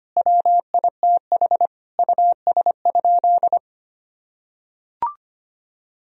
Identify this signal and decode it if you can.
Morse code, heard as a keyed tone, about 25 wpm, 700 Hz. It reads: WITH US?
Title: Sentences from Top 1000 Words - Review Part 3 - 25wpm